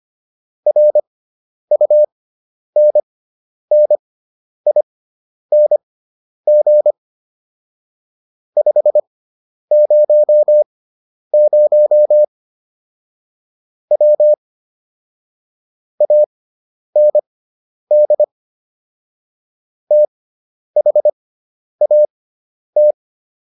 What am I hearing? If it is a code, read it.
Morse code, heard as a keyed tone, about 25 wpm, spaced as for 10 wpm, 600 Hz. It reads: RUNNING 500 W AND THAT